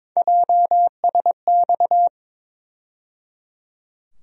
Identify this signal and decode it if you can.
Morse code, heard as a keyed tone, about 22 wpm, 700 Hz. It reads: JSX
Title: USA Call Sign Postfixes - 3 Letters 22wpm